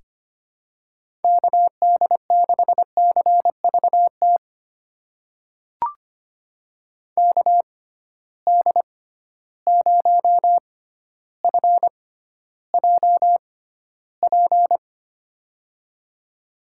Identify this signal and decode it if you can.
Morse code, heard as a keyed tone, about 25 wpm, spaced as for 13 wpm, 700 Hz. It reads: KD6CVT KD0FJP
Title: USA Call Signs - 6 Characters - 6x Character Spacing - 25wpm